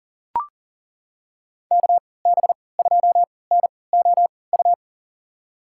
Morse code, heard as a keyed tone, about 37 wpm, spaced as for 22 wpm, 700 Hz. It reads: KB2NOU
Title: USA Call Signs - 3x Character Spacing - 40wpm